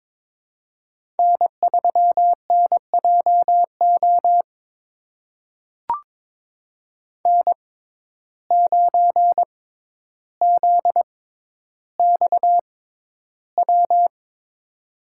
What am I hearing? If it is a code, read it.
Morse code, heard as a keyed tone, about 22 wpm, spaced as for 11 wpm, 700 Hz. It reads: N3NJO N9ZXW
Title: USA Call Signs - 5 Characters - 6x Character Spacing - 22wpm